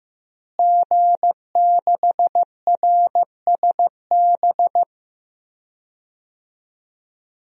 Morse code, heard as a keyed tone, about 15 wpm, 700 Hz. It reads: G6RSB
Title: DXCC-100 Call Signs - 5 Characters - Repeated 3x - 15wpm